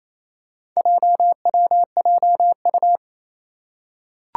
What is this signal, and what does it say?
Morse code, heard as a keyed tone, about 27 wpm, 700 Hz. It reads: JWJU
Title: Four Challenging Letter-Number Combinations 28wpm